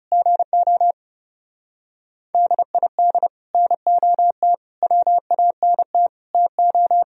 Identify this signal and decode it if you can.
Morse code, heard as a keyed tone, about 37 wpm, 700 Hz. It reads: GO DID NOT WANT TO
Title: Speed-Racing - Sets of 5 Words 30wpm